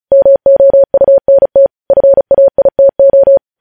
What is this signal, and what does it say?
Morse code, heard as a keyed tone, about 36 wpm, 550 Hz. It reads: MOUNT FAITO
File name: podcast / News headlines-1x-35x35